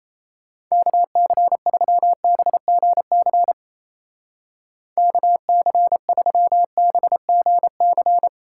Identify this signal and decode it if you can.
Morse code, heard as a keyed tone, about 37 wpm, 700 Hz. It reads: KC3BGC KC3BGC